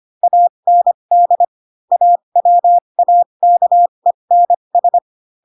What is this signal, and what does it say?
Morse code, heard as a keyed tone, about 25 wpm, spaced as for 22 wpm, 700 Hz. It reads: AND AWAKENS
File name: Pinocchio / Pinocchio ch6 18 25 ik1hge